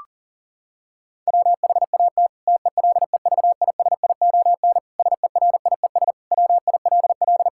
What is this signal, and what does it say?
Morse code, heard as a keyed tone, about 39 wpm, 700 Hz. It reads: WHAT TELEVISION SERIES WILL
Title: Sentences from Top 1000 Words - Review Part 5 - 40wpm